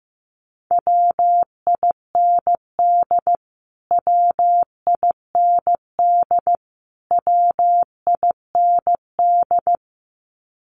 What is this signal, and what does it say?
Morse code, heard as a keyed tone, about 15 wpm, 700 Hz. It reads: WIND WIND WIND
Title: Top 500 Words Repeated 15wpm